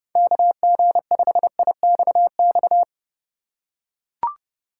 Morse code, heard as a keyed tone, about 30 wpm, 700 Hz. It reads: KG5IXX
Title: USA Call Signs - 6 Characters - 3x Character Spacing - 30wpm